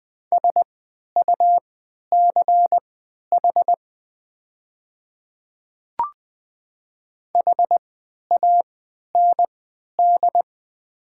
Morse code, heard as a keyed tone, about 20 wpm, spaced as for 11 wpm, 700 Hz. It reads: SUCH HAND
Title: 3x Character Spacing - Top 200 Words - 20wpm